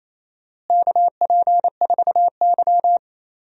Morse code, heard as a keyed tone, about 28 wpm, 700 Hz. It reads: KP4Y
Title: DXCC-100 Call Signs - 4 Characters - Repeated 3x - 28wpm